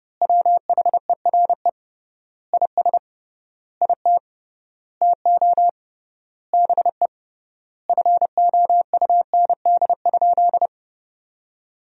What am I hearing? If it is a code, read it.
Morse code, heard as a keyed tone, about 30 wpm, 700 Hz. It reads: WHERE IS IT TO BE FOUND?
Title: 3x Word-Spacing - Sentences from Top 300 Words - 30wpm